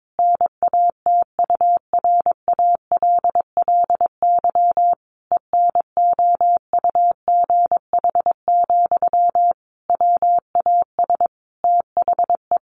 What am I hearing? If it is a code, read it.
Morse code, heard as a keyed tone, about 22 wpm, 700 Hz. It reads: NATURALLY ENOUGH, WAS THE
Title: Sentences from Top 1300 Words 22wpm